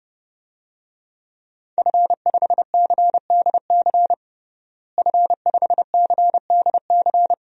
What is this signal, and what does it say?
Morse code, heard as a keyed tone, about 30 wpm, 700 Hz. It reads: F5CDC F5CDC
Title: DXCC-100 Call Signs - 5 Characters - Repeated 2x - 30wpm